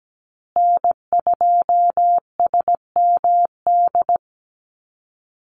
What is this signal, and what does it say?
Morse code, heard as a keyed tone, about 17 wpm, 700 Hz. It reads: N2SMD
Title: USA Call Signs Repeated 2x 17wpm